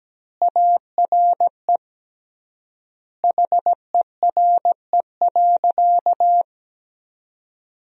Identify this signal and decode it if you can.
Morse code, heard as a keyed tone, about 17 wpm, 700 Hz. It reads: ARE HERE.
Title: 3x Word-Spacing - Sentences from Top 200 Words - 17wpm